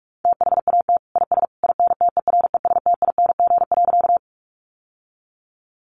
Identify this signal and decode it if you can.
Morse code, heard as a keyed tone, about 44 wpm, 700 Hz. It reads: THAT IS INTERESTING.
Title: Sentences from Top 1000 Words 45wpm